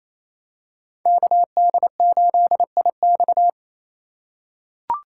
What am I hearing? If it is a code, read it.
Morse code, heard as a keyed tone, about 28 wpm, 700 Hz. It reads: KD8IX
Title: USA Call Signs - 4x Character Spacing - 28wpm